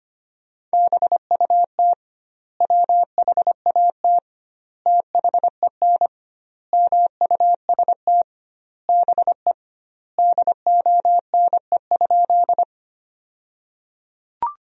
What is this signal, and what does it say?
Morse code, heard as a keyed tone, about 25 wpm, 700 Hz. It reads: BUT WHAT THEN MUST BE DONE?
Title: 2x Word-Spacing - Sentences from Top 500 Words - 25wpm